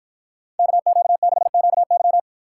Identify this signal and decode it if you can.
Morse code, heard as a keyed tone, about 52 wpm, 700 Hz. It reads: KQ6QY